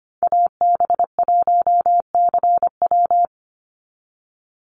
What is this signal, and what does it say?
Morse code, heard as a keyed tone, about 25 wpm, 700 Hz. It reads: AB1CW
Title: Call Signs Repeated 2x - 25wpm